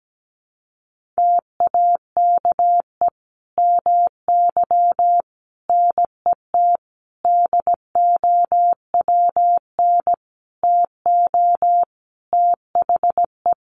Morse code, heard as a keyed tone, about 17 wpm, 700 Hz. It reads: TAKE MY NET DOWN TO THE
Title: Sentences from Top 1400 Words 17wpm